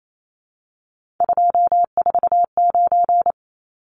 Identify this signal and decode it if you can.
Morse code, heard as a keyed tone, about 28 wpm, 700 Hz. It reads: 249
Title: Numbers - 3-Digit 28wpm